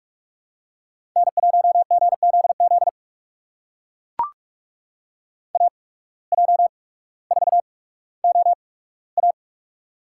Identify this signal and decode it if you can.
Morse code, heard as a keyed tone, about 42 wpm, spaced as for 19 wpm, 700 Hz. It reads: N1GZZ AJ4OA
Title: USA Call Signs - 8x Character Spacing - 45wpm